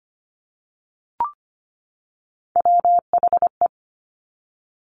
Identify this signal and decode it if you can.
Morse code, heard as a keyed tone, about 25 wpm, 700 Hz. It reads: WHE
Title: Top Three Letter Combinations in Words 25wpm